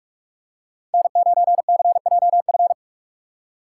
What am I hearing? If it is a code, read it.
Morse code, heard as a keyed tone, about 46 wpm, 700 Hz. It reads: N9CJF
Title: USA Call Signs - 5 Characters - 2x Character Spacing - 45wpm